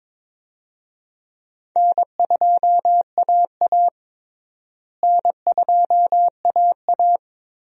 Morse code, heard as a keyed tone, about 22 wpm, 700 Hz. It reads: N2AA N2AA
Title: USA Call Signs - 4 Characters - Repeated 3x - 22wpm